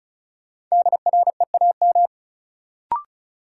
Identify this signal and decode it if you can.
Morse code, heard as a keyed tone, about 34 wpm, 700 Hz. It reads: DREAM
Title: Sentences from Top 1500 Words - New Word Drill Alt - 35wpm